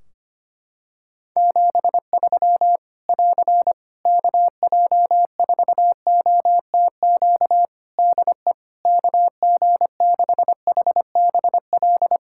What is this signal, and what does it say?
Morse code, heard as a keyed tone, about 25 wpm, 700 Hz. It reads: 73 <AR> KJ4OTQ DE KG6HBL
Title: Practice QSO - Exchange 4 - Ending - 25wpm